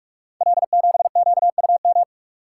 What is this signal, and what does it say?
Morse code, heard as a keyed tone, about 44 wpm, 690 Hz. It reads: L7QUM
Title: Code Groups - 5 Letter:Numbers - 45wpm